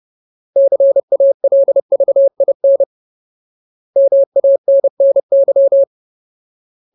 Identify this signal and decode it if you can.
Morse code, heard as a keyed tone, about 30 wpm, 550 Hz. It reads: CALVIN MANNY
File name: Ultimate Word List / 30WPM Ultimate